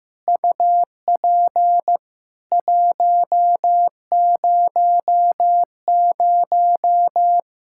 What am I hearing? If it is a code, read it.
Morse code, heard as a keyed tone, about 15 wpm, 700 Hz. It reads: UP 100
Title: Practice QSO - Exchange 2 - Info - 15wpm